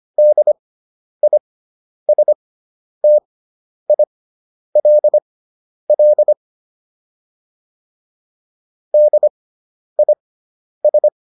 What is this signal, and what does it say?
Morse code, heard as a keyed tone, about 25 wpm, spaced as for 10 wpm, 600 Hz. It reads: DISTILL DIS